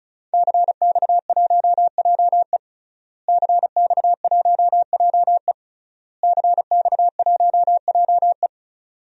Morse code, heard as a keyed tone, about 36 wpm, 700 Hz. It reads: CX1JE CX1JE CX1JE
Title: DXCC-100 Call Signs - 5 Characters - Repeated 3x - 35wpm